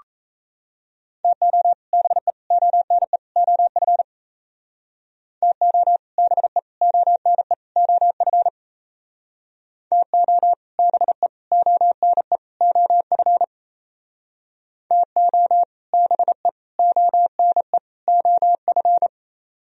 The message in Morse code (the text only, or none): TO BE ONE OF TO BE ONE OF TO BE ONE OF TO BE ONE OF